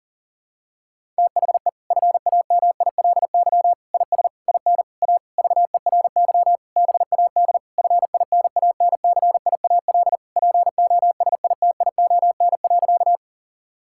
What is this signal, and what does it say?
Morse code, heard as a keyed tone, about 40 wpm, 700 Hz. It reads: THE FAMILY IS IN A VERY BAD FINANCIAL POSITION.
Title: Sentences from Top 1000 Words - Review Part 5 - 40wpm